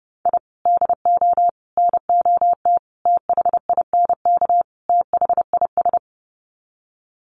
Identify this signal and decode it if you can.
Morse code, heard as a keyed tone, about 30 wpm, 700 Hz. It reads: I DO NOT THINK THIS
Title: Sets of 5 Words Encore 30wpm